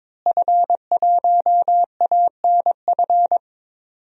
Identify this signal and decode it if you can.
Morse code, heard as a keyed tone, about 21 wpm, 700 Hz. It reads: F1ANF
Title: DXCC-100 Call Signs - 5 Characters - Repeated 3x - 22wpm